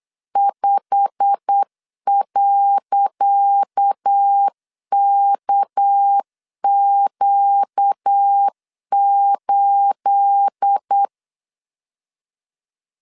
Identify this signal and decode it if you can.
Morse code, heard as a keyed tone, about 8 wpm, 800 Hz. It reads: 5.KQ8
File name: LEZIONI / 10 SIGNS / 10C DOT 8wpm 3649